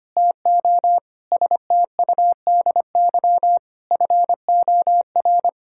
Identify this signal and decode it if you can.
Morse code, heard as a keyed tone, about 25 wpm, 700 Hz. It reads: TO STUDY FOR
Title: Sentences from Top 1000 Words - Review Part 5 - 25wpm